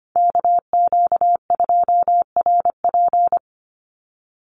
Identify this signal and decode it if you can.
Morse code, heard as a keyed tone, about 25 wpm, 700 Hz. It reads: KQ2RP
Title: NA SOTA Chaser Callsigns 25wpm